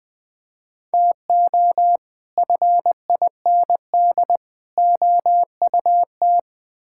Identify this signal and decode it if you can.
Morse code, heard as a keyed tone, about 20 wpm, 700 Hz. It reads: TO FIND OUT
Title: Build A Phrase - 3-Word Phrases - 20wpm